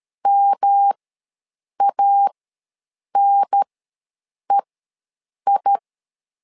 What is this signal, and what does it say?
Morse code, heard as a keyed tone, about 13 wpm, spaced as for 7 wpm, 800 Hz. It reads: MANEI
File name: LEZIONI / 3 MWRO / 3 MWRO ws 12wpm 1914